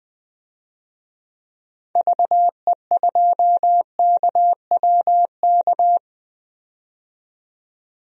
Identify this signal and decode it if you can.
Morse code, heard as a keyed tone, about 20 wpm, 700 Hz. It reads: VE2KWK
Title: DXCC-100 Call Signs - 6 Characters - 20wpm